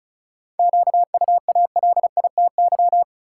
Morse code, heard as a keyed tone, about 36 wpm, 700 Hz. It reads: QUALITY
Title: Sentences from Top 1000 Words - Review Part 4 - New Word Drill - 35wpm